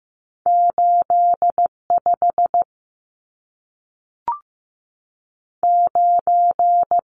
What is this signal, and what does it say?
Morse code, heard as a keyed tone, about 15 wpm, 700 Hz. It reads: 8 5 9
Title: Numbers - 2-Digit 15wpm